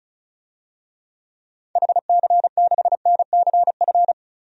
Morse code, heard as a keyed tone, about 34 wpm, 700 Hz. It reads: HC6NCF
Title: DXCC-100 Call Signs - 6 Characters - Repeated 3x - 35wpm